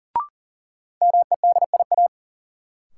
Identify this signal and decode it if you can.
Morse code, heard as a keyed tone, about 40 wpm, 700 Hz. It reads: MEDIA
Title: Sentences from Top 1000 Words - Review Part 4 - New Word Drill - 40wpm